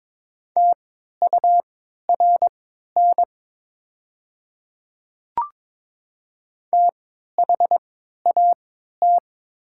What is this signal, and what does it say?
Morse code, heard as a keyed tone, about 22 wpm, spaced as for 13 wpm, 700 Hz. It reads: TURN THAT
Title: 3x Character Spacing - Top 500 Words - 22wpm